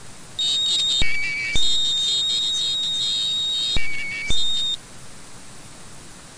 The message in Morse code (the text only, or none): none